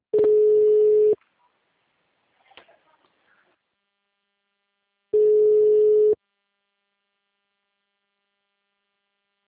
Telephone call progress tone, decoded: ringback tone

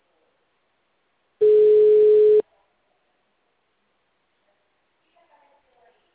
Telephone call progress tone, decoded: ringback tone

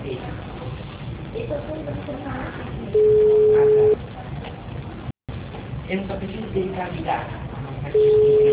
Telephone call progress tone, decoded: ringback tone

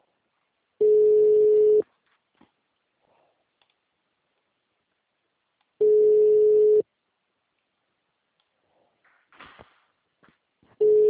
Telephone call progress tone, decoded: ringback tone

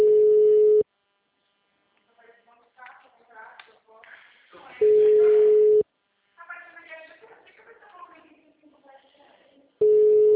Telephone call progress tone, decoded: ringback tone

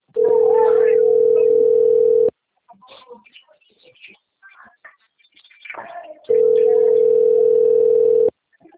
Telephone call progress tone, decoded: ringback tone

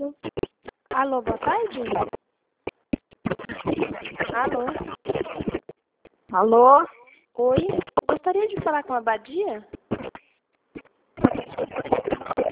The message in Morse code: none

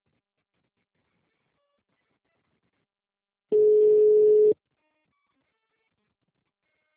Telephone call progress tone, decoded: ringback tone

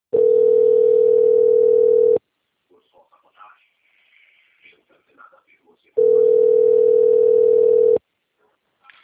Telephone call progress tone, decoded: ringback tone

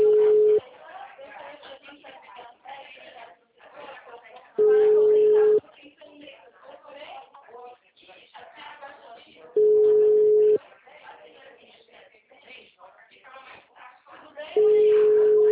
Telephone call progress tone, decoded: ringback tone

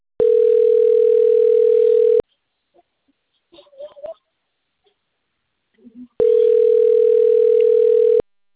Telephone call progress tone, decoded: ringback tone

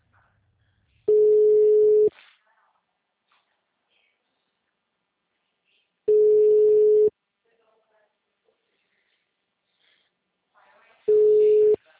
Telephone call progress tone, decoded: ringback tone

